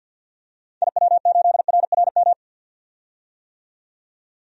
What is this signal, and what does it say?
Morse code, heard as a keyed tone, about 48 wpm, 700 Hz. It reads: IW8RRM
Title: DXCC-100 Call Signs - 6 Characters - 50wpm